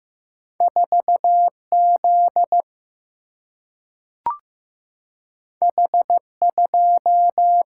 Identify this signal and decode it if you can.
Morse code, heard as a keyed tone, about 15 wpm, 700 Hz. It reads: 4 Z H 2
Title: Single-Letter Single-Number 15wpm